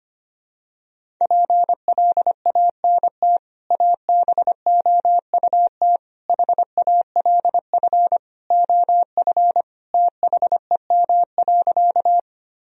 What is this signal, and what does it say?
Morse code, heard as a keyed tone, about 25 wpm, 700 Hz. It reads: PLANT ABOUT HALF OF THEM.
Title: Sentences from Top 1000 Words - Review Part 1 - 25wpm